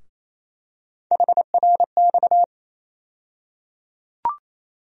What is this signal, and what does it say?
Morse code, heard as a keyed tone, about 28 wpm, 700 Hz. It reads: HRX
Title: USA Call Sign Postfixes - 3 Letters 28wpm